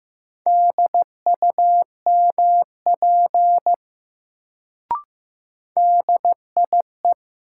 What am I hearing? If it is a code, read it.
Morse code, heard as a keyed tone, about 15 wpm, 700 Hz. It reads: DUMP DIE
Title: Sentences from Top 1300 Words - New Word Drill Alt - 15wpm